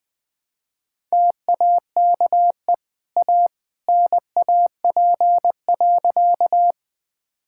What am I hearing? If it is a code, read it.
Morse code, heard as a keyed tone, about 21 wpm, 700 Hz. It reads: TAKE A NAP.
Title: Easy Sentences with Hints - 20wpm